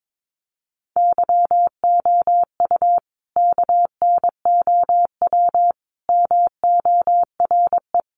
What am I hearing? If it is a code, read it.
Morse code, heard as a keyed tone, about 22 wpm, 700 Hz. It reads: YOU KNOW MORE
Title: Sentences from Top 100 Words 22wpm